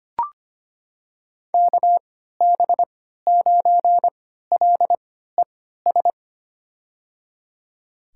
Morse code, heard as a keyed tone, about 25 wpm, spaced as for 14 wpm, 700 Hz. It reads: KB9LES
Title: USA Call Signs - 3x Character Spacing - 25wpm